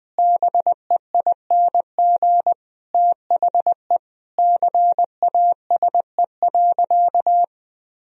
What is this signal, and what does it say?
Morse code, heard as a keyed tone, about 20 wpm, 700 Hz. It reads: BEING THE CASE.
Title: Sentences from Top 1000 Words - Review Part 3 - 20wpm